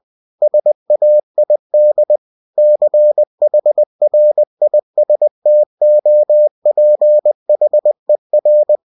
Morse code, heard as a keyed tone, about 20 wpm, 600 Hz. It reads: SAID CHRISTOPHER
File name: cw-books / winnie-the-pooh / winnie-20wpm-page-0097